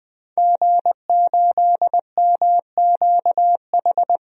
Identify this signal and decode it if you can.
Morse code, heard as a keyed tone, about 20 wpm, 700 Hz. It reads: G8MQH
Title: DXCC-100 Call Signs - 5 Characters - Repeated 3x - 20wpm